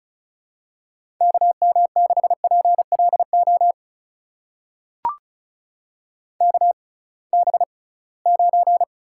Morse code, heard as a keyed tone, about 36 wpm, spaced as for 23 wpm, 700 Hz. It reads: KM6PLO KB9
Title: USA Call Signs - 6 Characters - 6x Character Spacing - 35wpm